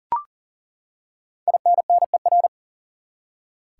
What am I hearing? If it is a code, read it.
Morse code, heard as a keyed tone, about 40 wpm, 700 Hz. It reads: INNER
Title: Five Letter Words - 40wpm